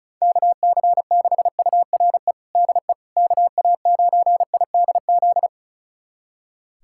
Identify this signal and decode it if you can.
Morse code, heard as a keyed tone, about 36 wpm, 700 Hz. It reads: KC6URE DE KA9IDZ
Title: USA Callsign DE Callsign - 6 Characters - 35wpm